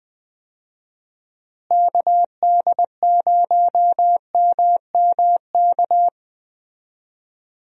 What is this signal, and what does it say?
Morse code, heard as a keyed tone, about 20 wpm, 700 Hz. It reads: KD0MMK
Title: USA Call Signs - 6 Characters - Repeated 3x - 20wpm